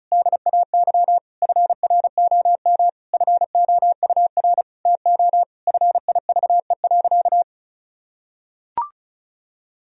Morse code, heard as a keyed tone, about 34 wpm, 700 Hz. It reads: DAY FROM FOUR TO FIVE.
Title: Sentences from Top 1000 Words - Review Part 2 - 35wpm